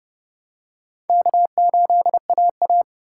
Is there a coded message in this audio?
Yes: Morse code from 1.1 to 2.8 s, heard as a keyed tone, about 28 wpm, 700 Hz. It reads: K8AA